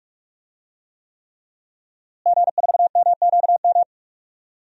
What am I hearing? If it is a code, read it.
Morse code, heard as a keyed tone, about 46 wpm, 700 Hz. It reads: G4MQM
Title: DXCC-100 Call Signs - 5 Characters - Repeated 3x - 45wpm